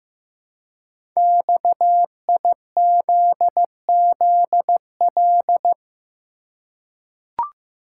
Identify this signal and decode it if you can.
Morse code, heard as a keyed tone, about 15 wpm, 700 Hz. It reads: XIZZL
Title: Code Groups - 5 Letters - 15wpm